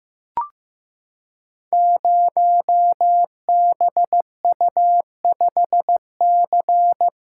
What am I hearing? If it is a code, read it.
Morse code, heard as a keyed tone, about 15 wpm, 700 Hz. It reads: 0BU5C
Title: Code Groups - 5 Letter:Numbers - 15wpm